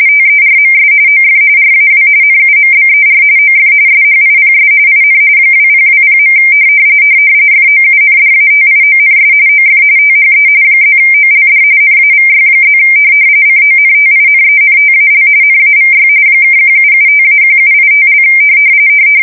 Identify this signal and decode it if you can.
RTTY, about 45.45 baud. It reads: TERMS OF RAW NUMBER OF SHIPS IN 2020. FOUNDATIONS OF THE +OLD NAVY+ CONTINENTAL NAVY (1775-1785)
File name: fsk170 / usnavy1